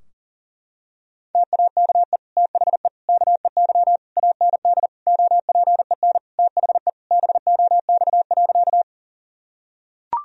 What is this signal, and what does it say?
Morse code, heard as a keyed tone, about 40 wpm, 700 Hz. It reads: TAKE THE KEY AND OPEN THE BOX.
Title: Sentences from Top 1000 Words - Review Part 4 - 40wpm